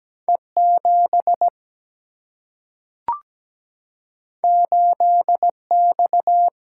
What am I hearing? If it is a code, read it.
Morse code, heard as a keyed tone, about 17 wpm, 700 Hz. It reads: E 7 8 X
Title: Single-Letter Single-Number 17wpm